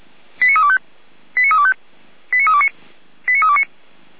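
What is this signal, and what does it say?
Five-tone selective call (ZVEI), heard as quick runs of five tones, tones of about 70 ms. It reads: 89325 89325 89329 89329
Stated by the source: ZVEI five-tone selective call 89333, received via SDR.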